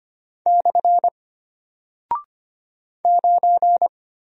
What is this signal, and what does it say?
Morse code, heard as a keyed tone, about 25 wpm, 700 Hz. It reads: / 9
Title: Single Character (Letter:Number:Punctuation:Prosign) Alt 25wpm